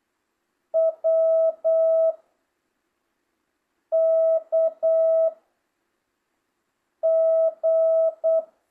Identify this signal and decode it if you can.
Morse code, heard as a keyed tone, about 8 wpm, 650 Hz. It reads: W K G